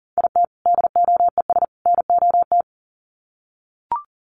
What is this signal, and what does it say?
Morse code, heard as a keyed tone, about 40 wpm, 700 Hz. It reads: IT DOES NOT
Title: Sets of 3 Words 40wpm